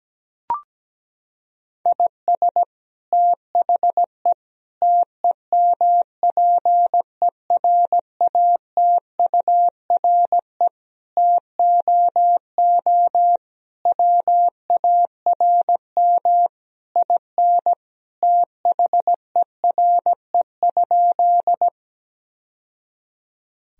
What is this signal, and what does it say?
Morse code, heard as a keyed tone, about 17 wpm, 700 Hz. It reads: IS THE TEMPERATURE TOO WARM IN THERE?
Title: Sentences from Top 1000 Words - Review Part 4 - 17wpm